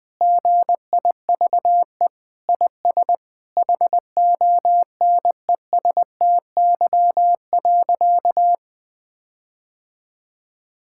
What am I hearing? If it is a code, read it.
Morse code, heard as a keyed tone, about 20 wpm, 700 Hz. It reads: GIVE IS HONESTY.
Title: Sentences with 8-Word Phrases - 20wpm